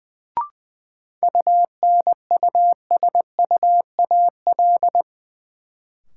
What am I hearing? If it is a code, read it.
Morse code, heard as a keyed tone, about 20 wpm, 700 Hz. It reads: UNUSUAL
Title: Sentences from Top 1400 Words - New Word Drill - 20wpm